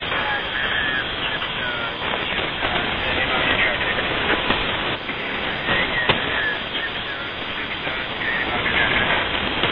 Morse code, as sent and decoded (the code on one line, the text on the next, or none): none